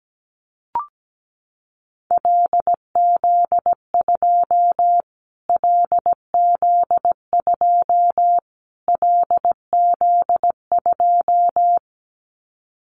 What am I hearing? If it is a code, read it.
Morse code, heard as a keyed tone, about 17 wpm, 700 Hz. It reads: LZ2 LZ2 LZ2
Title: Callsign Prefixes for Top 25 DXCC - Repeated 3x - 17wpm